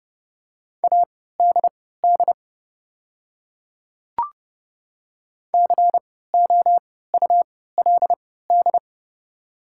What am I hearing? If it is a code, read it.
Morse code, heard as a keyed tone, about 32 wpm, spaced as for 17 wpm, 690 Hz. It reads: ADD COULD
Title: 3x Character Spacing - Top 200 Words - 30wpm